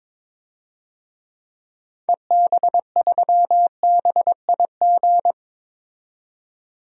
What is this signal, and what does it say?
Morse code, heard as a keyed tone, about 22 wpm, 700 Hz. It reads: EB3BIG
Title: DXCC-100 Call Signs - 6 Characters - 22wpm